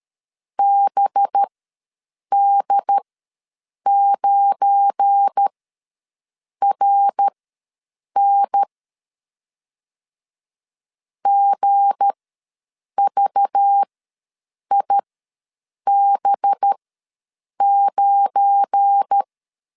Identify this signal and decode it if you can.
Morse code, heard as a keyed tone, about 13 wpm, spaced as for 7 wpm, 800 Hz. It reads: BD9RN GVIB9
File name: LEZIONI / 7 GF9Z / 7 GF9 ws 12wpm 3344